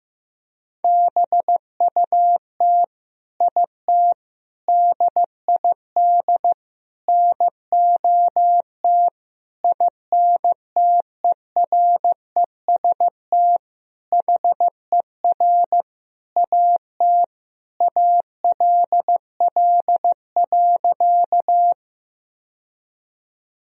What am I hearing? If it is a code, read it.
Morse code, heard as a keyed tone, about 15 wpm, 700 Hz. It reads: BUT IT DID NOT INTEREST HER AT ALL.
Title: Sentences from Top 1000 Words - Review Part 2 - 15wpm